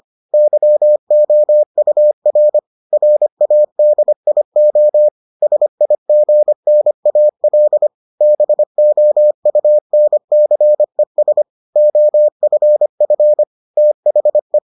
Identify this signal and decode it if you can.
Morse code, heard as a keyed tone, about 25 wpm, 600 Hz. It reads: YOUR RADIO SIGNAL BOUNCES OFF THE